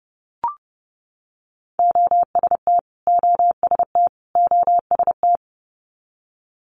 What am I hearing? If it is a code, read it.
Morse code, heard as a keyed tone, about 32 wpm, 700 Hz. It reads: OST OST OST
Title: Multi-letter Phonemes Repeated 3x 30wpm